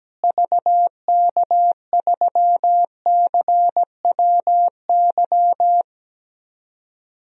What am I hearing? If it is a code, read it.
Morse code, heard as a keyed tone, about 17 wpm, 700 Hz. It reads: VK3CWY
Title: DXCC-100 Call Signs - 6 Characters - Repeated 2x - 17wpm